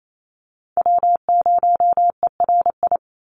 Morse code, heard as a keyed tone, about 28 wpm, 700 Hz. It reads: W0ERI
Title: NA SOTA Chaser Callsigns 28wpm